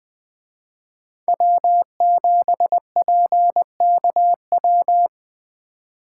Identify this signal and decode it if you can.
Morse code, heard as a keyed tone, about 21 wpm, 700 Hz. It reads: W7PKW
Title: Speed Racing - USA 1x3 Call Signs 17wpm